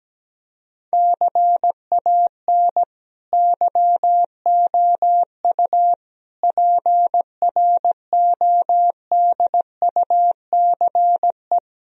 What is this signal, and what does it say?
Morse code, heard as a keyed tone, about 17 wpm, 700 Hz. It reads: CAN YOU PRODUCE
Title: Sentences from Top 1000 Words - Review Part 3 - 17wpm